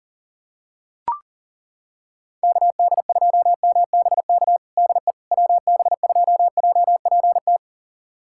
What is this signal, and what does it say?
Morse code, heard as a keyed tone, about 40 wpm, 700 Hz. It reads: KD2MBK DE WB2JPT
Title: USA Callsign DE Callsign 40wpm